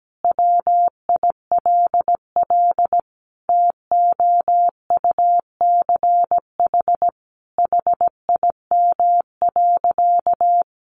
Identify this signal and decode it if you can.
Morse code, heard as a keyed tone, about 17 wpm, 700 Hz. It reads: WILL TOUCH HIM.
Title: Sentences from Top 1300 Words 17wpm